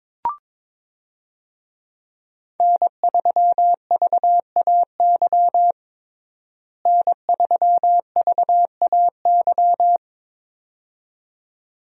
Morse code, heard as a keyed tone, about 22 wpm, 700 Hz. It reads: N3VAY N3VAY
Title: USA Call Signs - 5 Characters - Repeated 2x - 22wpm